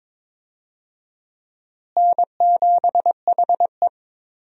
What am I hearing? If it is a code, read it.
Morse code, heard as a keyed tone, about 21 wpm, 700 Hz. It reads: N7HE
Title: USA Call Signs - 4 Characters - Repeated 2x - 22wpm